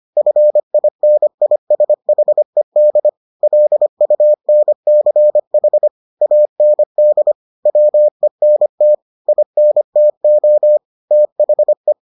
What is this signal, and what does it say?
Morse code, heard as a keyed tone, about 25 wpm, 600 Hz. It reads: FINISHED LUNCH AND WENT INTO THE